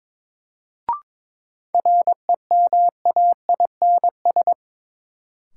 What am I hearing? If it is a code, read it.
Morse code, heard as a keyed tone, about 22 wpm, 700 Hz. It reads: REMAINS
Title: Sets of 5 Words Encore - New Word Drill - 22wpm